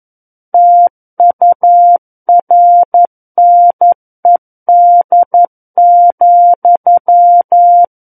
Morse code, heard as a keyed tone, about 11 wpm, 700 Hz. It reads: TURNED,
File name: Book One / 02-Chapter 2 - 11 WPM